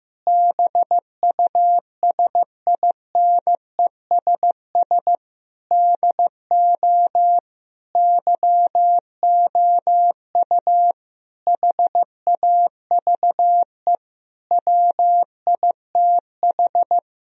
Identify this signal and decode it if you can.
Morse code, heard as a keyed tone, about 15 wpm, 700 Hz. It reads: BUSINESS DO YOU HAVE WITH
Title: Sentences from Top 1000 Words - Review Part 3 - 15wpm